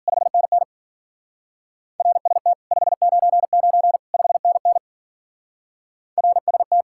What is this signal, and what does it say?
Morse code, heard as a keyed tone, about 51 wpm, 700 Hz. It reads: 5NN RST 599 5NN RST